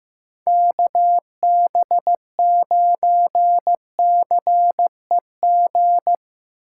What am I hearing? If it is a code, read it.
Morse code, heard as a keyed tone, about 15 wpm, 700 Hz. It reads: KB9CEG
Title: USA Call Signs - 4x Character Spacing - 15wpm